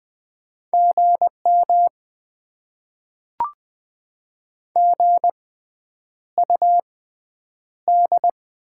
Morse code, heard as a keyed tone, about 20 wpm, spaced as for 9 wpm, 700 Hz. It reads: GM GUD
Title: Top 20 QSO Elements - 6x Character Spacing - 20wpm